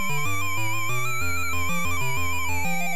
There is no code.